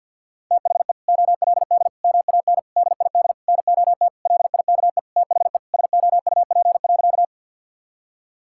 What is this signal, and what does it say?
Morse code, heard as a keyed tone, about 46 wpm, 700 Hz. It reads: THE OLD MAN DID NOT LIKE THE SOUP.
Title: Sentences from Top 1000 Words - Review Part 6 - 50wpm